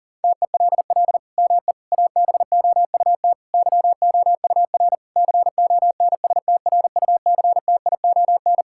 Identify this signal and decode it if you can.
Morse code, heard as a keyed tone, about 37 wpm, 700 Hz. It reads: TELL ME ABOUT YOUR CONSTRUCTION
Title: Sentences from Top 1600 Words 40wpm